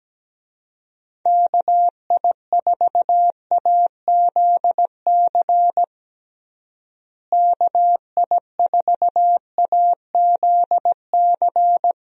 Morse code, heard as a keyed tone, about 17 wpm, 700 Hz. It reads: KI4AZC KI4AZC
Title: USA Call Signs - 6 Characters - Repeated 3x - 17wpm